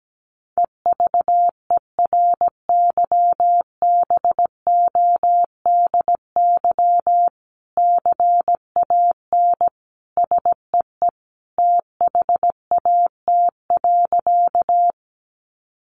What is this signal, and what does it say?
Morse code, heard as a keyed tone, about 17 wpm, 700 Hz. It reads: EVERYBODY CAN SEE THAT.
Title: Sentences from Top 1400 Words 17wpm